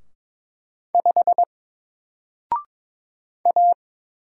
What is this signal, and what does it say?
Morse code, heard as a keyed tone, about 22 wpm, 700 Hz. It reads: 5 A
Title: Single Letter-Number Alt 22wpm